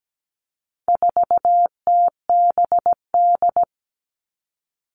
Morse code, heard as a keyed tone, about 17 wpm, 700 Hz. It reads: 4TBD
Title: All Cut Numbers - 4-Digit 17wpm